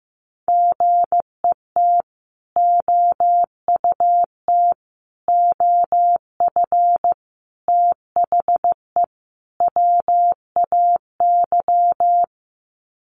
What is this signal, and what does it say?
Morse code, heard as a keyed tone, about 15 wpm, 700 Hz. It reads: GET OUT OF THE WAY
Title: Sets of 5 Words Encore 15wpm